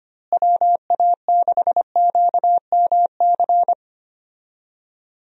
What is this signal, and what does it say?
Morse code, heard as a keyed tone, about 25 wpm, 700 Hz. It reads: WA6QMC
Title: USA Call Signs - 6 Characters - Repeated 2x - 25wpm